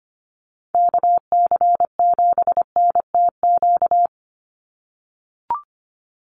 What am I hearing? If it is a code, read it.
Morse code, heard as a keyed tone, about 25 wpm, 700 Hz. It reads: KC7NTQ
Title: USA Call Signs 25wpm